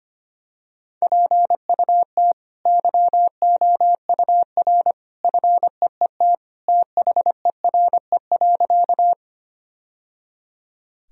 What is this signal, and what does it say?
Morse code, heard as a keyed tone, about 25 wpm, 700 Hz. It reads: PUT YOUR FEET THERE.
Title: Sentences from Top 1000 Words - Review Part 1 - 25wpm